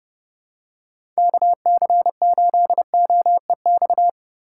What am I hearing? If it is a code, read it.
Morse code, heard as a keyed tone, about 29 wpm, 700 Hz. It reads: KC8OEX
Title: USA Call Signs - 6 Characters - 4x Character Spacing - 30wpm